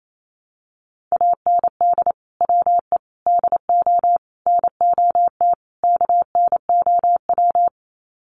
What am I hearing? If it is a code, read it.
Morse code, heard as a keyed tone, about 28 wpm, 700 Hz. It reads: AND WE DO NOT KNOW
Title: Sets of 5 Words Encore 28wpm